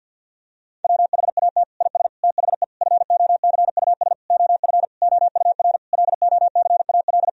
Morse code, heard as a keyed tone, about 50 wpm, 700 Hz. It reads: WHAT IS THE FOCUS OF OUR LOCAL
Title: Sentences from Top 1000 Words - Review Part 4 - 50wpm